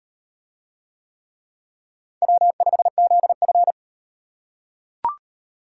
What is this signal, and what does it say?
Morse code, heard as a keyed tone, about 36 wpm, 700 Hz. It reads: W5ZF